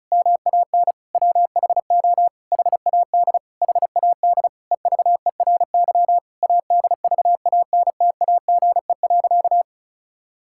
Morse code, heard as a keyed tone, about 36 wpm, 700 Hz. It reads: MAN WHO HAD HAD EVERY ADVANTAGE.
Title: Sentences from Top 1000 Words - Review Part 6 - 35wpm